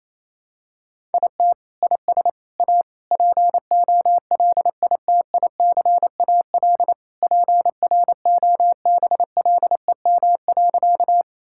Morse code, heard as a keyed tone, about 28 wpm, 700 Hz. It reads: IT IS A POLITICAL PROBLEM.
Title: Sentences from Top 1000 Words - Review Part 5 - 28wpm